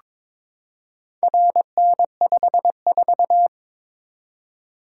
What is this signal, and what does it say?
Morse code, heard as a keyed tone, about 22 wpm, 700 Hz. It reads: RN54
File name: practice / 22wpm / Four Letter-Number Combinations - 22wpm